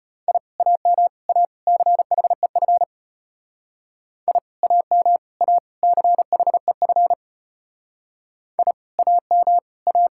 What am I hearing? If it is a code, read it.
Morse code, heard as a keyed tone, about 33 wpm, 700 Hz. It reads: I AM A CHEF I AM A CHEF I AM A